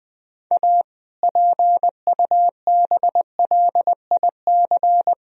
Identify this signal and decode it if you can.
Morse code, heard as a keyed tone, about 20 wpm, 700 Hz. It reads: A PUBLIC
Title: Sentences from Top 1000 Words - Review Part 3 - 20wpm